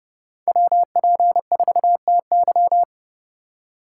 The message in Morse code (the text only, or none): WP4TY